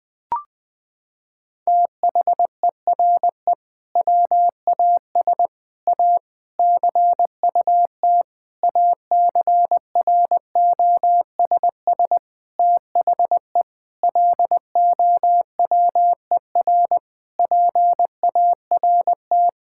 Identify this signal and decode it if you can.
Morse code, heard as a keyed tone, about 21 wpm, 700 Hz. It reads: THERE WAS A CUT ACROSS THE LOWER PART
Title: Sentences from Top 1000 Words - Review Part 5 - 20wpm